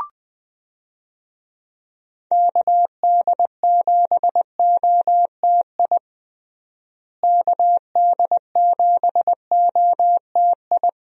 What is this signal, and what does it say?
Morse code, heard as a keyed tone, about 20 wpm, 700 Hz. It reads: KD7OTI KD7OTI
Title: USA Call Signs - 6 Characters - Repeated 2x - 20wpm